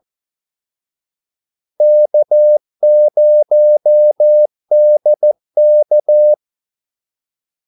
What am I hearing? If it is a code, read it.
Morse code, heard as a keyed tone, about 14 wpm, 600 Hz. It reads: K0DK
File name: Callsigns-14wpm